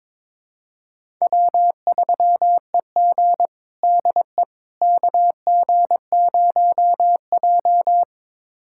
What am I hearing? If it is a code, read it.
Morse code, heard as a keyed tone, about 22 wpm, 700 Hz. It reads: W3EG DE KG0J
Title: USA Callsign DE Callsign - 4 Characters - 22wpm